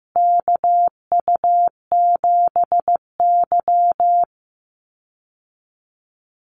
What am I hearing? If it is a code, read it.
Morse code, heard as a keyed tone, about 15 wpm, 700 Hz. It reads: KU7Y
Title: Call Signs Repeated 2x - 15wpm